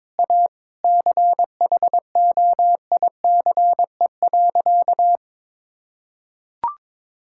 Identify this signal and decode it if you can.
Morse code, heard as a keyed tone, about 22 wpm, 700 Hz. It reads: A CHOICE.
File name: practice / 22wpm / Sentences with 3-Word Phrases - 22wpm